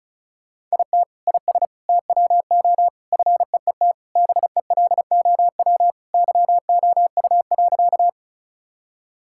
Morse code, heard as a keyed tone, about 34 wpm, 700 Hz. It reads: IT IS TWO FEET BELOW YOU.
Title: Sentences from Top 1000 Words - Review Part 5 - 35wpm